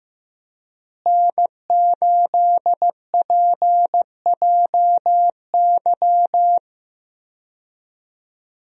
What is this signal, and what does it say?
Morse code, heard as a keyed tone, about 15 wpm, 700 Hz. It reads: N8PJY
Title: USA Call Signs - 5 Characters - Repeated 3x - 15wpm